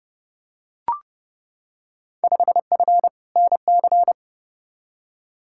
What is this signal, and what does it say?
Morse code, heard as a keyed tone, about 30 wpm, 700 Hz. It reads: 5F NC
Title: Field Day Exchanges Encore - 30wpm